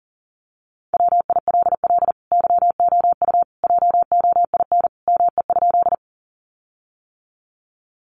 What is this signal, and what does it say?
Morse code, heard as a keyed tone, about 40 wpm, 700 Hz. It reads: WILL YOU JOIN ME?
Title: Sentences from Top 1400 Words 40wpm